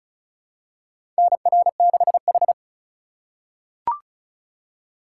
Morse code, heard as a keyed tone, about 34 wpm, 700 Hz. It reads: NR6H
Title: USA Call Signs - 4 Characters - 3x Character Spacing - 35wpm